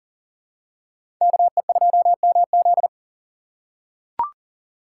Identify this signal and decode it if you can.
Morse code, heard as a keyed tone, about 40 wpm, 700 Hz. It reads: KE2MZ
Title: USA Call Signs - 5 Characters - 4x Character Spacing - 40wpm